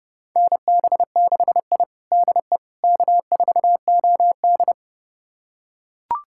Morse code, heard as a keyed tone, about 32 wpm, 700 Hz. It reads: NB6I DE K4OD
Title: USA Callsign DE Callsign - 4 Characters - 30wpm